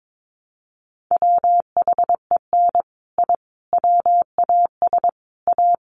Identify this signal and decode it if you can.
Morse code, heard as a keyed tone, about 22 wpm, 700 Hz. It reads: WHEN I WAS A
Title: Sets of 5 Words 22wpm